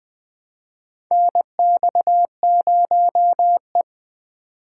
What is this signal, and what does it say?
Morse code, heard as a keyed tone, about 20 wpm, 700 Hz. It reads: NX0E
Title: USA Call Signs - 4 Characters - 3x Character Spacing - 20wpm